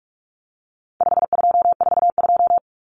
Morse code, heard as a keyed tone, about 44 wpm, 700 Hz. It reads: 5242
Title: Numbers - 4-Digit 45wpm